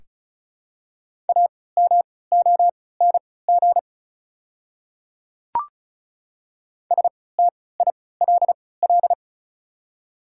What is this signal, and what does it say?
Morse code, heard as a keyed tone, about 34 wpm, spaced as for 20 wpm, 690 Hz. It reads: AMONG STILL
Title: 3x Character Spacing - Top 500 Words - 35wpm